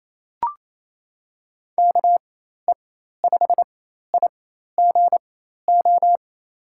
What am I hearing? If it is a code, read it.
Morse code, heard as a keyed tone, about 28 wpm, spaced as for 13 wpm, 700 Hz. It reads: KE5IGO
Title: USA Call Signs - 6 Characters - 4x Character Spacing - 28wpm